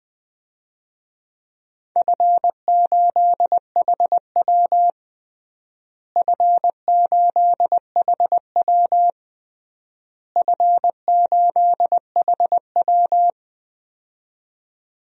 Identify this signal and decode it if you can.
Morse code, heard as a keyed tone, about 21 wpm, 700 Hz. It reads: F8HW F8HW F8HW
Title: DXCC-100 Call Signs - 4 Characters - Repeated 3x - 20wpm